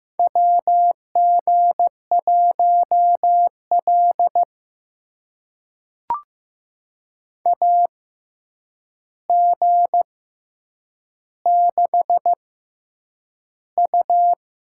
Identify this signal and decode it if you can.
Morse code, heard as a keyed tone, about 15 wpm, spaced as for 8 wpm, 700 Hz. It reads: WG1L AG6U
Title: USA Call Signs - 4 Characters - 6x Character Spacing - 15wpm